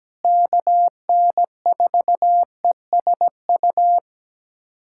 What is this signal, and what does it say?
Morse code, heard as a keyed tone, about 17 wpm, 700 Hz. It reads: KN4ESU
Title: USA Callsign DE Callsign 17wpm